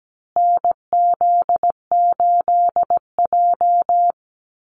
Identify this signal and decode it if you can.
Morse code, heard as a keyed tone, about 17 wpm, 700 Hz. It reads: NZ8J
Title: Call Signs Repeated 2x - 17wpm